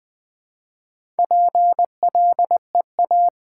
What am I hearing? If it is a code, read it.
Morse code, heard as a keyed tone, about 20 wpm, 700 Hz. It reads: PLEA
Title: Build A Word - 1 Letter At A Time - 3 to 5-Letter Words - 20wpm